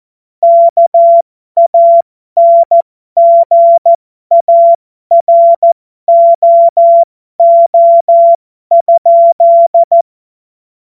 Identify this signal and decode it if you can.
Morse code, heard as a keyed tone, about 14 wpm, spaced as for 12 wpm, 680 Hz. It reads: KANGAROO?
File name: Morsekurs / Witze